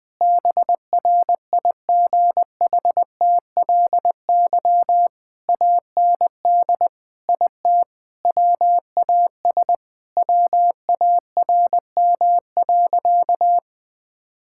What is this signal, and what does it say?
Morse code, heard as a keyed tone, about 20 wpm, 700 Hz. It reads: BRIGHTLY AND IT WAS WARM.
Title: Sentences with 2-Word Phrases - 20wpm